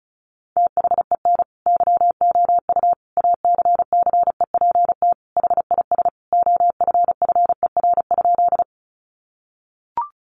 Morse code, heard as a keyed tone, about 34 wpm, 700 Hz. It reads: THEN YOU ACCEPT HIS OFFER?
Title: Sentences from Top 1300 Words 35wpm